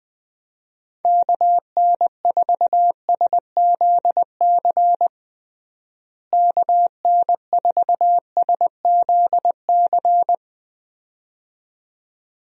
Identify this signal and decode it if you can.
Morse code, heard as a keyed tone, about 20 wpm, 700 Hz. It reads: KN4SZC KN4SZC
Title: USA Call Signs - 6 Characters - Repeated 2x - 20wpm